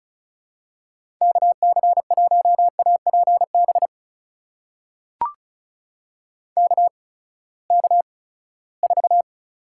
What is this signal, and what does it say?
Morse code, heard as a keyed tone, about 36 wpm, 700 Hz. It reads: K C 1 A P B K K 4
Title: USA Call Signs - 6 Characters - 8x Character Spacing - 35wpm